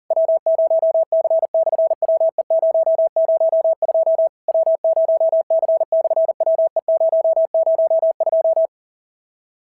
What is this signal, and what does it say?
Morse code, heard as a keyed tone, about 40 wpm, 650 Hz. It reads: W0C/WE002 W0C/WE002
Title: SOTA QSO Practice 40wpm